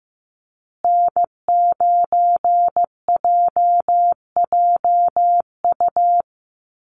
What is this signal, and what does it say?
Morse code, heard as a keyed tone, about 15 wpm, 700 Hz. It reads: N9JJU
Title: USA Call Signs Repeated 3x 15wpm